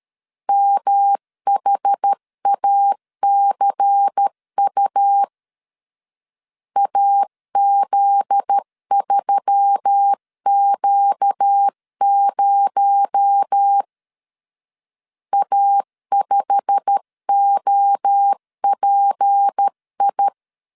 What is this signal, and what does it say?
Morse code, heard as a keyed tone, about 13 wpm, 800 Hz. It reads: MHACU AZ3Q0 A5OPI
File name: LEZIONI / 8 YJ45 / 8 YJ45 12wpm 2151